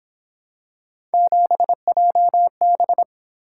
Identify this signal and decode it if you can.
Morse code, heard as a keyed tone, about 27 wpm, 700 Hz. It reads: 7JB
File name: Speed-Racing - Three Letter-Number Combinations - 22wpm